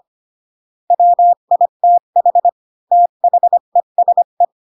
Morse code, heard as a keyed tone, about 25 wpm, 700 Hz. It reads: WITH THESE
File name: Pinocchio / Pinocchio ch1 20 25 ik1hge